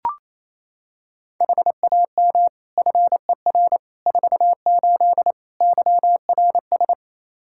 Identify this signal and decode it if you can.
Morse code, heard as a keyed tone, about 28 wpm, 700 Hz. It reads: HAM FER 48 YRS
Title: K1USN SST QSO Practice 28wpm